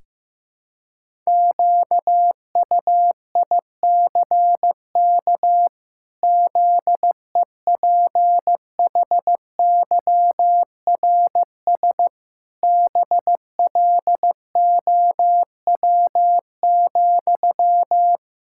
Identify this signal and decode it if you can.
Morse code, heard as a keyed tone, about 15 wpm, 700 Hz. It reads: QUICK ZEPHYRS BLOW,
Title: Pangram Challenge v2 - 15wpm